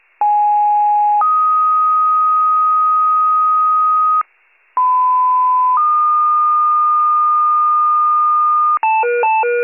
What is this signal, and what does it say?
Two-tone page, heard as a one-second tone, then a three-second tone, about 830 then 1250 Hz, about 1010 then 1220 Hz. It is heard from 0.2 to 8.8 s.